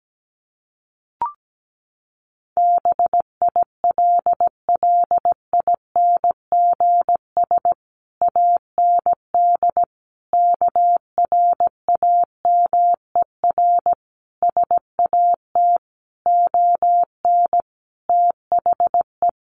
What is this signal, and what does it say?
Morse code, heard as a keyed tone, about 17 wpm, 700 Hz. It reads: BILLINGS AND KRAMER SAT ON THE
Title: A Sound Of Thunder 17wpm